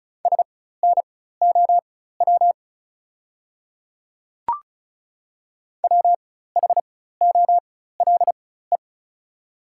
Morse code, heard as a keyed tone, about 36 wpm, spaced as for 16 wpm, 700 Hz. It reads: SNOW WHOLE
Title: 4x Character Spacing - Top 500 Words - 35wpm